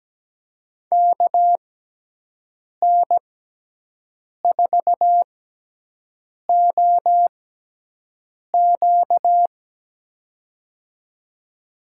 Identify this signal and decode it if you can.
Morse code, heard as a keyed tone, about 17 wpm, spaced as for 6 wpm, 700 Hz. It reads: KN4OQ
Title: USA Call Signs - 5 Characters - 6x Character Spacing - 17wpm